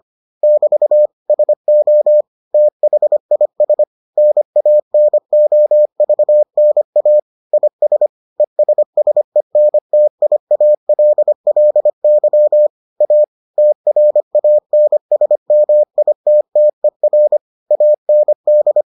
Morse code, heard as a keyed tone, about 25 wpm, 600 Hz. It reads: <BT> SO THIS NANOVNA IS ESSENTIALLY A TRANSMITTER AND